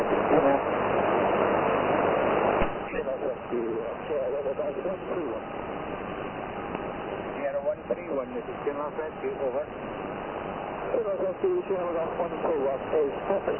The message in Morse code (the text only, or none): none